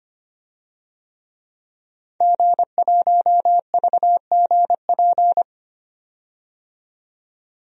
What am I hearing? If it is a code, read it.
Morse code, heard as a keyed tone, about 25 wpm, 700 Hz. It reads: G1VGP
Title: DXCC-100 Call Signs - 5 Characters - 25wpm